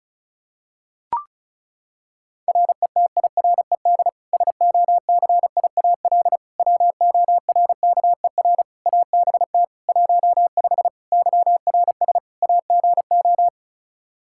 Morse code, heard as a keyed tone, about 36 wpm, 700 Hz. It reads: RETIRED SOCIAL WORKER ABT 15 YRS AGO
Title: K1USN SST QSO Practice 35wpm